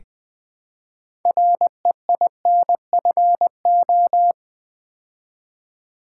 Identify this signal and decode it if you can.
Morse code, heard as a keyed tone, about 20 wpm, 700 Hz. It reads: REINFO
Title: Build A Word - 3 Letters At A Time - 9-Letter Words - 20wpm